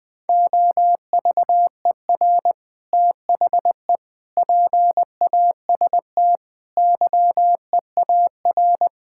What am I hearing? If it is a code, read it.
Morse code, heard as a keyed tone, about 21 wpm, 700 Hz. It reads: OVER THE PAST YEAR
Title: Sentences with 3-Word Phrases - 20wpm